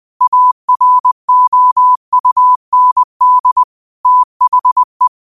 Morse code, heard as a keyed tone, about 20 wpm, 1000 Hz. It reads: AROUND THE
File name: qotdfiles / 20wpm